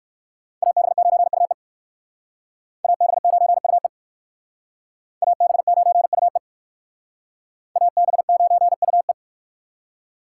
Morse code, heard as a keyed tone, about 56 wpm, 700 Hz. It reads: AB9UE AB9UE AB9UE AB9UE